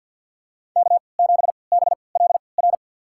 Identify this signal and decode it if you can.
Morse code, heard as a keyed tone, about 51 wpm, spaced as for 28 wpm, 690 Hz. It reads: K7BLR
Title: USA Call Signs - 5 Characters - 3x Character Spacing - 50wpm